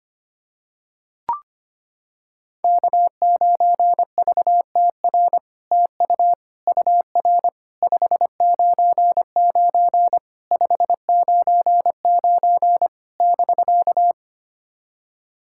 Morse code, heard as a keyed tone, about 25 wpm, 700 Hz. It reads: K9VTR TU UR 599 599 <BK>
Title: SOTA QSO Practice 25wpm